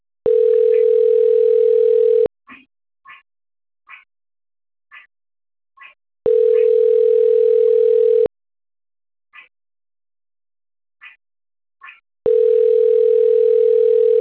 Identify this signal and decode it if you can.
Ringback tone (North America), 440 + 480 Hz.